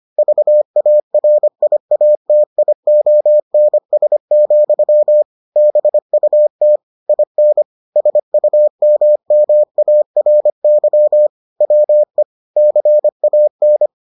Morse code, heard as a keyed tone, about 25 wpm, 600 Hz. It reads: VARIATIONS, BUT IN SUMMARY WE CAN